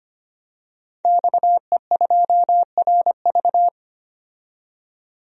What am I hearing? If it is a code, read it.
Morse code, heard as a keyed tone, about 25 wpm, 700 Hz. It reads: XE2RV
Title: DXCC-100 Call Signs - 5 Characters - 25wpm